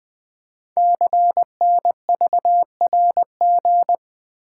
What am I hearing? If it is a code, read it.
Morse code, heard as a keyed tone, about 20 wpm, 700 Hz. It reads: CNVRG
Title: Code Groups - 5 Letters - 20wpm